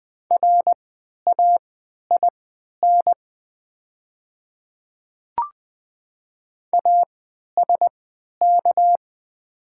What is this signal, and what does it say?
Morse code, heard as a keyed tone, about 20 wpm, spaced as for 11 wpm, 700 Hz. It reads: RAIN ASK
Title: 3x Character Spacing - Top 300 Words - 20wpm